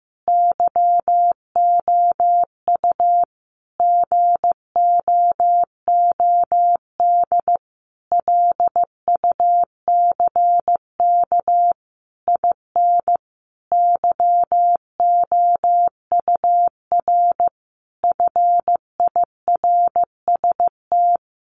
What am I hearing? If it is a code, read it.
Morse code, heard as a keyed tone, about 15 wpm, 700 Hz. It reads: YOU GOOD LUCK IN YOUR FIRST
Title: Sentences from Top 1500 Words 15wpm